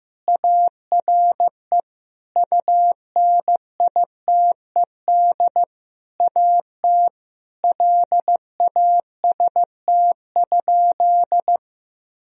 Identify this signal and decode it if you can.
Morse code, heard as a keyed tone, about 16 wpm, 700 Hz. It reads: ARE UNITED AT LAST?
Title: Sentences from Top 1000 Words - Review Part 4 - 15wpm